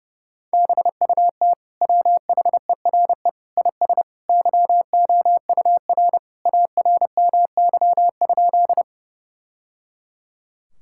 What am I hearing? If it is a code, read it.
Morse code, heard as a keyed tone, about 30 wpm, 700 Hz. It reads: BUT WHERE IS YOUR ARMY?
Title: Sentences from Top 1000 Words - Review Part 5 - 30wpm